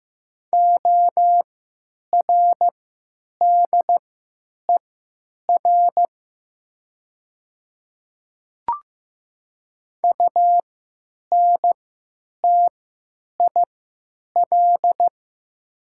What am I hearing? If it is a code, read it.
Morse code, heard as a keyed tone, about 15 wpm, spaced as for 9 wpm, 700 Hz. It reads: ORDER UNTIL
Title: 3x Character Spacing - Top 500 Words - 15wpm